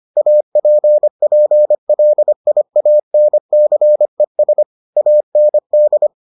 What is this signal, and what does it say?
Morse code, heard as a keyed tone, about 25 wpm, 600 Hz. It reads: APPLIANCES AND